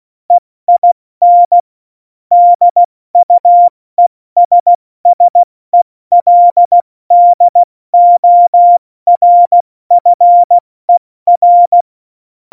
Morse code, heard as a keyed tone, about 16 wpm, 700 Hz. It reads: EIN DUESSELDORFER
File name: Morsekurs / Witze / Witz 10.Dezember 2019, 70 F80